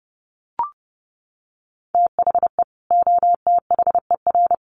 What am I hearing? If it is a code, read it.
Morse code, heard as a keyed tone, about 30 wpm, 700 Hz. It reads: THE OTHER
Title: Sets of 3 Words 30wpm